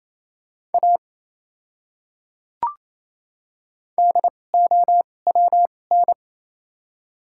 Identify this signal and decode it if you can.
Morse code, heard as a keyed tone, about 28 wpm, spaced as for 20 wpm, 700 Hz. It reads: A DOWN
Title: 2x Character Spacing - Top 200 Words - 28wpm